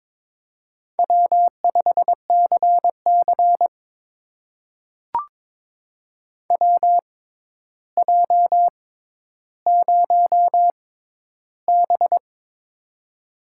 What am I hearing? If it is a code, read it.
Morse code, heard as a keyed tone, about 22 wpm, spaced as for 11 wpm, 700 Hz. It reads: W5CC WJ0B
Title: USA Call Signs - 6x Character Spacing - 22wpm